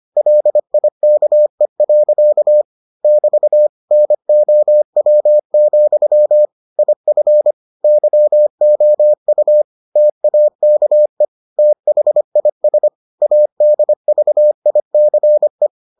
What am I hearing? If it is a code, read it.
Morse code, heard as a keyed tone, about 25 wpm, 600 Hz. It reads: LIKE. <BT> NOW, IF YOU TAKE THIS ADVICE